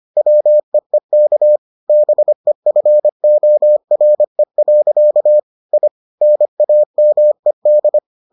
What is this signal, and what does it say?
Morse code, heard as a keyed tone, about 25 wpm, 600 Hz. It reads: WEEK BEFORE. I NAMED